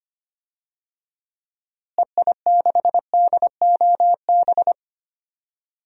Morse code, heard as a keyed tone, about 25 wpm, 700 Hz. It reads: EI6DOB
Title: DXCC-100 Call Signs - 6 Characters - Repeated 3x - 25wpm